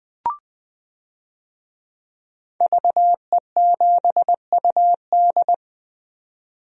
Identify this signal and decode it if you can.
Morse code, heard as a keyed tone, about 20 wpm, 700 Hz. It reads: VE7UD
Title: DXCC-100 Call Signs - 5 Characters - 20wpm